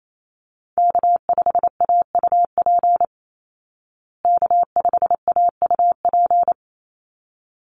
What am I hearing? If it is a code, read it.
Morse code, heard as a keyed tone, about 28 wpm, 700 Hz. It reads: K5AUP K5AUP
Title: Call Signs Repeated 3x - 28wpm